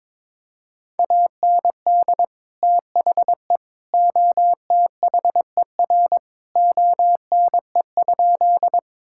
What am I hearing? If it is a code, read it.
Morse code, heard as a keyed tone, about 22 wpm, 700 Hz. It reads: AND THE OTHER ONE?
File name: practice / 22wpm / Sentences with 3-Word Phrases - 22wpm